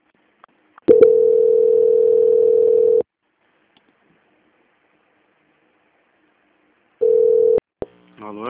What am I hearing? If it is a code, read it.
Ringback tone (North America), 440 + 480 Hz, with speech.
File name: out- -2015- - - .16